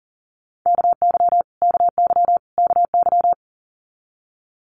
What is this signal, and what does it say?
Morse code, heard as a keyed tone, about 41 wpm, 700 Hz. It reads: KY KY KY
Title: Multi-letter Phonemes Repeated 3x 40wpm